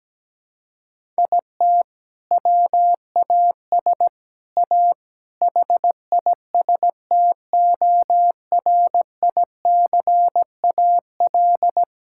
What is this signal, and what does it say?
Morse code, heard as a keyed tone, about 17 wpm, 700 Hz. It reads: IT WAS A HISTORICAL
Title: Sentences from Top 1000 Words - Review Part 5 - 17wpm